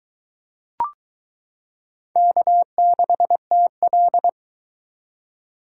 Morse code, heard as a keyed tone, about 22 wpm, 700 Hz. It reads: K6TL